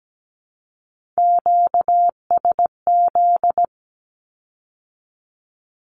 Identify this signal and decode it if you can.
Morse code, heard as a keyed tone, about 17 wpm, 700 Hz. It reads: QSZ
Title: Q-Codes with Spoken Meaning - 17wpm